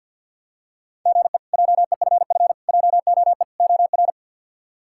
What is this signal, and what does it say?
Morse code, heard as a keyed tone, about 52 wpm, 700 Hz. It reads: GE JEFF JOE OR
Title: K1USN SST QSO Practice - 50wpm